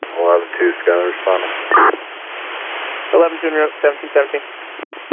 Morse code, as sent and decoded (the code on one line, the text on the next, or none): none